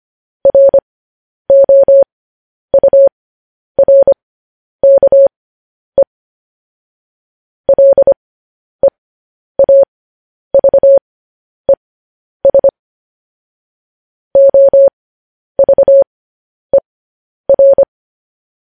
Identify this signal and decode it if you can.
Morse code, heard as a keyed tone, about 25 wpm, spaced as for 10 wpm, 550 Hz. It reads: ROURKE LEAVES OVER
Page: news headlines in morse code at 10 wpm